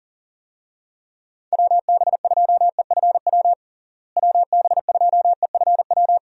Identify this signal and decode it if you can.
Morse code, heard as a keyed tone, about 40 wpm, 700 Hz. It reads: WB2EFW WB2EFW
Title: USA Call Signs - 6 Characters - Repeated 3x - 40wpm